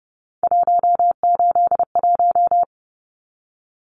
Morse code, heard as a keyed tone, about 30 wpm, 700 Hz. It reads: 181
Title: Numbers - 3-Digit 30wpm